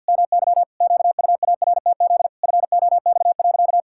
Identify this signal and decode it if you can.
Morse code, heard as a keyed tone, about 47 wpm, 700 Hz. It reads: MY QUARTZ FOX.
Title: Pangram Challenge v2 - 50wpm